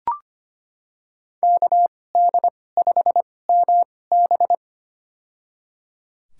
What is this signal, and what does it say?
Morse code, heard as a keyed tone, about 25 wpm, spaced as for 18 wpm, 700 Hz. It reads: KD5MB
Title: USA Call Signs - 5 Characters - 2x Character Spacing - 25wpm